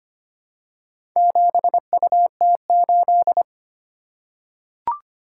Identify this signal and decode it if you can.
Morse code, heard as a keyed tone, about 25 wpm, 700 Hz. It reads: 7UT8
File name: practice / 25wpm / Four Letter-Number Combinations - 25wpm